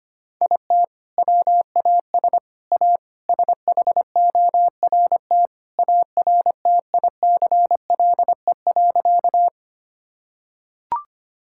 Morse code, heard as a keyed tone, about 25 wpm, 700 Hz. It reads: IT WAS A SHORT ARTICLE.
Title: Sentences from Top 1000 Words - Review Part 5 - 25wpm